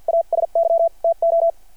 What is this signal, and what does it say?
Morse code, heard as a keyed tone, about 52 wpm, spaced as for 43 wpm, 660 Hz. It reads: ASY TO
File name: archived / CRSnet